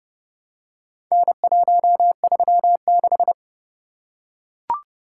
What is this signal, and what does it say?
Morse code, heard as a keyed tone, about 28 wpm, 700 Hz. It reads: N136